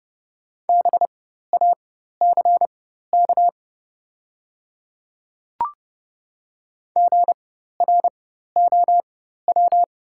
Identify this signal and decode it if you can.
Morse code, heard as a keyed tone, about 30 wpm, spaced as for 14 wpm, 690 Hz. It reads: BACK GROW
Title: 4x Character Spacing - Top 300 Words - 30wpm